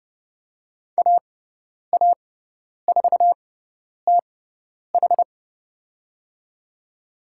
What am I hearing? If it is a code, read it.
Morse code, heard as a keyed tone, about 30 wpm, spaced as for 10 wpm, 700 Hz. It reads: AA4TH
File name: Lesson 16g - Letter c - Call Signs 30wpm